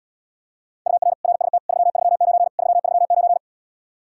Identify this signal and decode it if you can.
Morse code, heard as a keyed tone, about 79 wpm, 700 Hz. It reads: UR RST 339 339